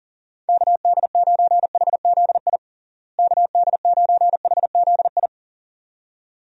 Morse code, heard as a keyed tone, about 40 wpm, 700 Hz. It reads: KD9HZI KD9HZI
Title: USA Call Signs - 6 Characters - Repeated 2x - 40wpm